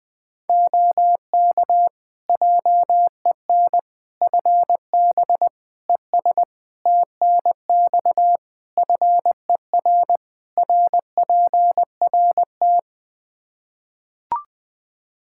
Morse code, heard as a keyed tone, about 21 wpm, 700 Hz. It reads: OK JEN FB ES TNX FER RPRT
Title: Practice QSO - Exchange 2 - Info - 20wpm